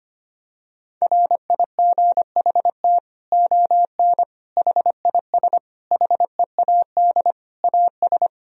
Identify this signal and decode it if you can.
Morse code, heard as a keyed tone, about 25 wpm, 700 Hz. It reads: RIGHT ON HIS HEAD AS
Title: Sentences from Top 1000 Words - Review Part 2 - 25wpm